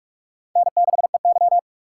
Morse code, heard as a keyed tone, about 41 wpm, 700 Hz. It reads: N6EY
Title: USA Call Signs - 4 Characters - 3x Character Spacing - 45wpm